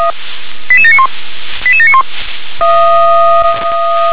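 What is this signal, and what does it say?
Five-tone selective call (ZVEI), heard as quick runs of five tones, tones of about 70 ms. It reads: 88791 88791
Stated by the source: ZVEI five-tone selective call 88765 received with an SDR.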